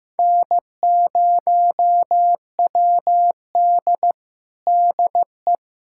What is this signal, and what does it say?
Morse code, heard as a keyed tone, about 16 wpm, 700 Hz. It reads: N0WD DE
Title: USA Callsign DE Callsign - 4 Characters - 15wpm